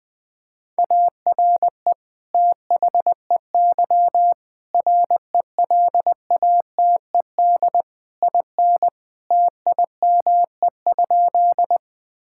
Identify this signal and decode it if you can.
Morse code, heard as a keyed tone, about 20 wpm, 700 Hz. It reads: ARE THEY RELATED IN TIME?
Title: Sentences from Top 1000 Words - Review Part 5 - 20wpm